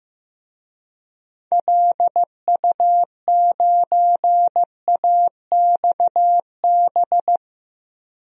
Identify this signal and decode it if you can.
Morse code, heard as a keyed tone, about 15 wpm, 700 Hz. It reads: LU9AXB
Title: DXCC-100 Call Signs - 6 Characters - Repeated 2x - 15wpm